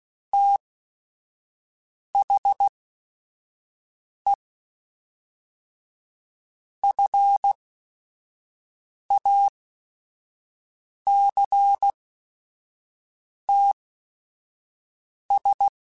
Morse code, heard as a keyed tone, about 16 wpm, spaced as for 5 wpm, 800 Hz. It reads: THE FACTS
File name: Quotes / Quotes-mp3s / IgnoringFacts @5